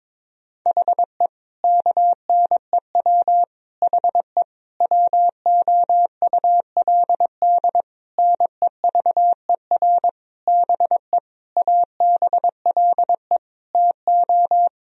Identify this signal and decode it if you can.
Morse code, heard as a keyed tone, about 22 wpm, 700 Hz. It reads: HE KNEW HE WOULD NEVER BE ABLE TO
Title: Sentences from Top 1000 Words - Review Part 2 - 22wpm